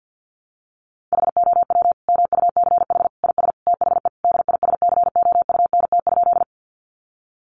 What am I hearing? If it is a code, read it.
Morse code, heard as a keyed tone, about 48 wpm, 700 Hz. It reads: HOW MUCH IS THE DISCOUNT?
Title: Sentences from Top 1200 Words 50wpm